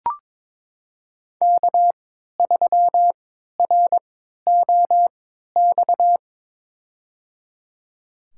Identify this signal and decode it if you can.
Morse code, heard as a keyed tone, about 22 wpm, spaced as for 12 wpm, 700 Hz. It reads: K3ROX
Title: USA Call Signs - 5 Characters - 3x Character Spacing - 22wpm